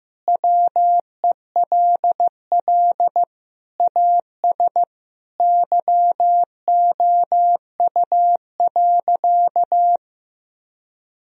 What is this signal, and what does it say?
Morse code, heard as a keyed tone, about 16 wpm, 700 Hz. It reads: WELL AS YOU.
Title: Sentences from Top 1000 Words - Review Part 3 - 15wpm